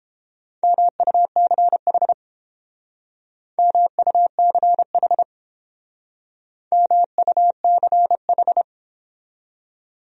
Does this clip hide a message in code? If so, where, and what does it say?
Yes: Morse code from 0.6 to 8.6 s, heard as a keyed tone, about 30 wpm, 700 Hz. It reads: MUCH MUCH MUCH